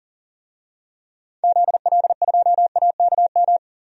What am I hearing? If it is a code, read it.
Morse code, heard as a keyed tone, about 40 wpm, 700 Hz. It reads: ZL2AKM
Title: DXCC-100 Call Signs - 6 Characters - 40wpm